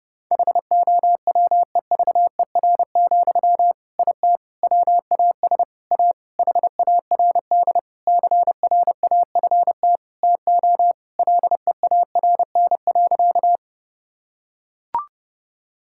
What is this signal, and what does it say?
Morse code, heard as a keyed tone, about 32 wpm, 700 Hz. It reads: HOWEVER, IT WAS A HARD CRAFT TO LEARN.
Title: Sentences from Top 1000 Words - Review Part 6 - 30wpm